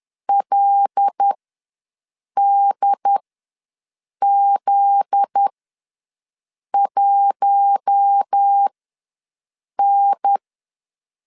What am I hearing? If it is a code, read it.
Morse code, heard as a keyed tone, about 10 wpm, spaced as for 6 wpm, 800 Hz. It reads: LDZ1N
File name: LEZIONI / 9 2 ws 10wpm 4130